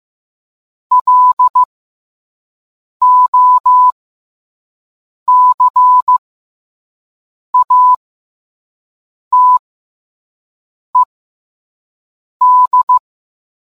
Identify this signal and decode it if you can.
Morse code, heard as a keyed tone, about 15 wpm, spaced as for 5 wpm, 1000 Hz. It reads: LOCATED